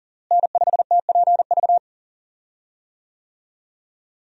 Morse code, heard as a keyed tone, about 40 wpm, 700 Hz. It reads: N5TPV
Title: USA Call Signs - 5 Characters - Repeated 2x - 40wpm